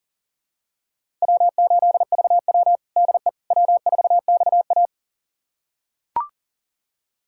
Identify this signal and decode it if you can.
Morse code, heard as a keyed tone, about 40 wpm, 700 Hz. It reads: W8VW DE W4XA
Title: USA Callsign DE Callsign - 4 Characters - 40wpm